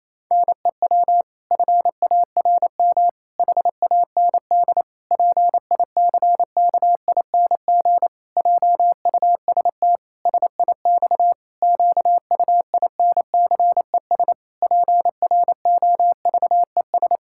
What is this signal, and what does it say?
Morse code, heard as a keyed tone, about 28 wpm, 700 Hz. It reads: NEW FARM HAND PICKING JUST SIX QUINCES PROVES
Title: Pangram Challenge v2 - 28wpm